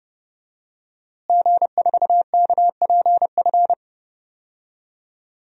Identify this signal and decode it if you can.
Morse code, heard as a keyed tone, about 30 wpm, 700 Hz. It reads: G4KPF
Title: DXCC-100 Call Signs - 5 Characters - Repeated 3x - 30wpm